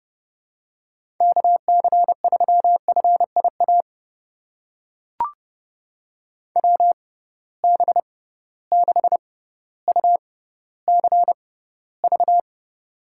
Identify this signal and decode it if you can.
Morse code, heard as a keyed tone, about 30 wpm, spaced as for 15 wpm, 700 Hz. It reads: KC3FIA WB6UCV
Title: USA Call Signs - 6 Characters - 6x Character Spacing - 30wpm